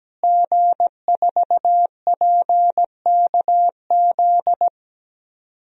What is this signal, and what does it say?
Morse code, heard as a keyed tone, about 17 wpm, 700 Hz. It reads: G4PKZ
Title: DXCC-100 Call Signs - 5 Characters - Repeated 3x - 17wpm